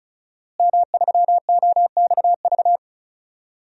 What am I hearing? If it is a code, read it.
Morse code, heard as a keyed tone, about 36 wpm, 700 Hz. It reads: M3OXV
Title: DXCC-100 Call Signs - 5 Characters - Repeated 3x - 35wpm